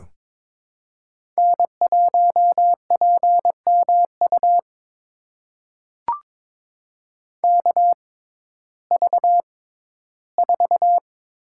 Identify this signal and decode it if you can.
Morse code, heard as a keyed tone, about 22 wpm, spaced as for 14 wpm, 700 Hz. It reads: N1PMU KV4
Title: USA Call Signs - 5 Characters - 6x Character Spacing - 22wpm